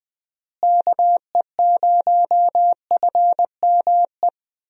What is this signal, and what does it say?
Morse code, heard as a keyed tone, about 20 wpm, 700 Hz. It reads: KE0FME
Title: USA Call Signs - 6 Characters - Repeated 3x - 20wpm